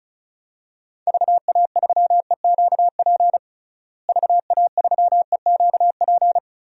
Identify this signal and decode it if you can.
Morse code, heard as a keyed tone, about 36 wpm, 700 Hz. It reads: VA3EQP VA3EQP
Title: DXCC-100 Call Signs - 6 Characters - Repeated 3x - 35wpm